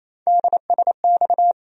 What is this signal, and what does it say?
Morse code, heard as a keyed tone, about 28 wpm, 700 Hz. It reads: DSX
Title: USA Call Sign Postfixes - 3 Letters 28wpm